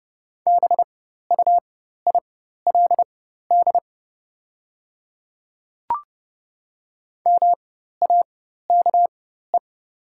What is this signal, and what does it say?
Morse code, heard as a keyed tone, about 30 wpm, spaced as for 14 wpm, 700 Hz. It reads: BUILD MAKE
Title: 4x Character Spacing - Top 200 Words - 30wpm